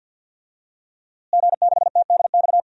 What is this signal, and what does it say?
Morse code, heard as a keyed tone, about 51 wpm, 700 Hz. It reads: G6TDX
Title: DXCC-100 Call Signs - 5 Characters - Repeated 2x - 50wpm